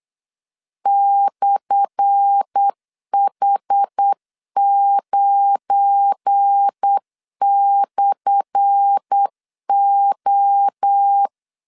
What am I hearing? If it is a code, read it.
Morse code, heard as a keyed tone, about 8 wpm, 800 Hz. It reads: /H9/O
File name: LEZIONI / 10 SIGNS / 10B INT 8wpm 3534